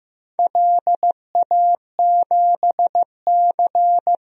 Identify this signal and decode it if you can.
Morse code, heard as a keyed tone, about 15 wpm, 700 Hz. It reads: LA7C
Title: DXCC-100 Call Signs - 4 Characters - Repeated 2x - 15wpm